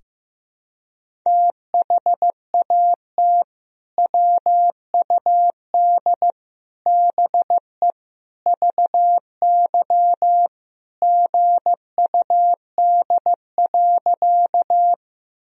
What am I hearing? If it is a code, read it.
Morse code, heard as a keyed tone, about 15 wpm, 700 Hz. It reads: THAT WUD BE VY GUD.
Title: Sentences from Top 200 Words and Abbreviations 15wpm